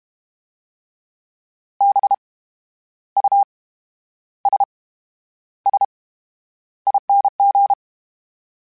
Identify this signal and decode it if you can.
Morse code, heard as a keyed tone, about 32 wpm, spaced as for 11 wpm, 800 Hz. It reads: BUSSING
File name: Special INGs / Special INGs-mp3s / VerySlowToFaster Ing @30